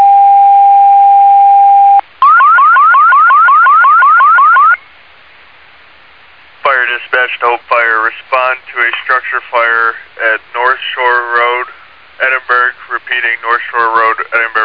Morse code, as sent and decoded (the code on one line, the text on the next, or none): none